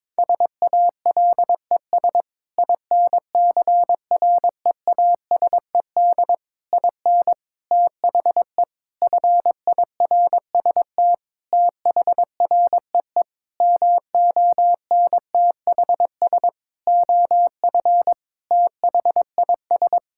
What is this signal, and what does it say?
Morse code, heard as a keyed tone, about 21 wpm, 700 Hz. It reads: SALES INCREASED IN THE FIRST THREE MONTHS OF THIS
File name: practice / 22wpm / Sentences with 8-Word Phrases - 22wpm